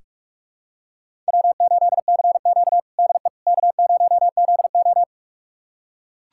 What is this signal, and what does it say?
Morse code, heard as a keyed tone, about 41 wpm, 700 Hz. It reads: W8CQ DE K0ZO
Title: USA Callsign DE Callsign - 4 Characters - 45wpm